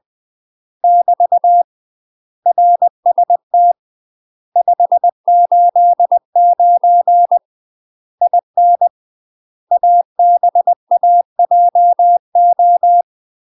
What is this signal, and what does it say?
Morse code, heard as a keyed tone, about 20 wpm, 700 Hz. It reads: <BT> RST 589 IN ABAJO
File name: UK MASTER QSOs TRAINING / 20WPM / UK MASTER QSOs 0037